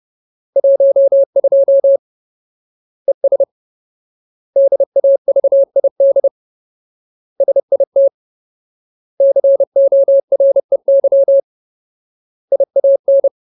Morse code, heard as a keyed tone, about 30 wpm, 550 Hz. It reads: 12 ES DAVID SIT COREY IAN
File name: Ultimate Word List / 30WPM Ultimate